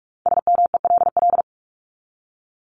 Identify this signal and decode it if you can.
Morse code, heard as a keyed tone, about 44 wpm, 700 Hz. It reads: SMELL
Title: Top 1501-1750 Words 45wpm